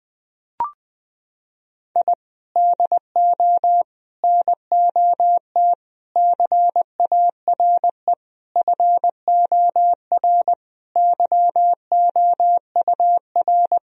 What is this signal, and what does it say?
Morse code, heard as a keyed tone, about 20 wpm, 700 Hz. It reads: I DO NOT CARE FOR YOUR
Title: Sentences from Top 1000 Words - Review Part 3 - 20wpm